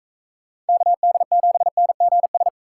Morse code, heard as a keyed tone, about 41 wpm, 700 Hz. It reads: KD7NGS